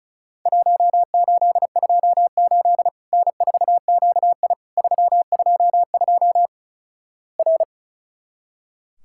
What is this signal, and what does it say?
Morse code, heard as a keyed tone, about 34 wpm, 700 Hz. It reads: 1828 N4QI 322 R
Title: Worked All Europe DX Contest - QTC Exchange - 35wpm